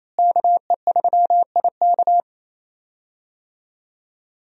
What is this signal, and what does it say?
Morse code, heard as a keyed tone, about 28 wpm, 700 Hz. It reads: KE3IK
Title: USA Call Signs - 5 Characters - Repeated 3x - 28wpm